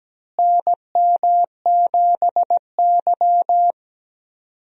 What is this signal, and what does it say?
Morse code, heard as a keyed tone, about 17 wpm, 700 Hz. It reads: NM7Y
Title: USA Call Signs - 4 Characters - Repeated 2x - 17wpm